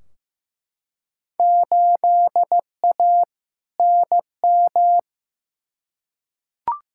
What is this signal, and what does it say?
Morse code, heard as a keyed tone, about 15 wpm, 700 Hz. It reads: 8A NM
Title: Field Day Exchanges Encore - 15wpm